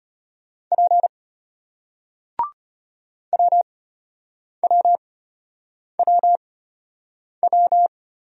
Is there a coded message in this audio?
Yes: Morse code from 0.7 to 7.9 s, heard as a keyed tone, about 33 wpm, spaced as for 8 wpm, 700 Hz. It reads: P WWWW